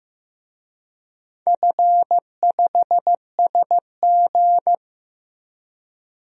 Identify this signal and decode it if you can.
Morse code, heard as a keyed tone, about 15 wpm, 700 Hz. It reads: F5SG
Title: DXCC-100 Call Signs - 4 Characters - Repeated 3x - 15wpm